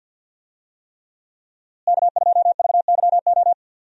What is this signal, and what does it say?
Morse code, heard as a keyed tone, about 48 wpm, 700 Hz. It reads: K2VYO
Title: USA Call Signs - 5 Characters - Repeated 3x - 50wpm